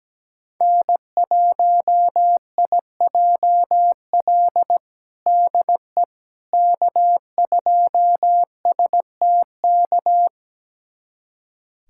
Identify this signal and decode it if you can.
Morse code, heard as a keyed tone, about 17 wpm, 700 Hz. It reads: N1IJL DE K2STK
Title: USA Callsign DE Callsign - 5 Characters - 17wpm